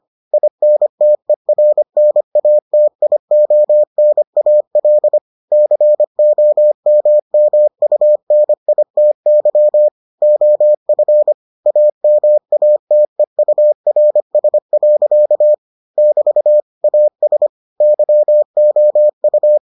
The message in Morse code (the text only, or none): INTERNATIONAL COMMUNITY OF AMATEURS. <BT> AS YOU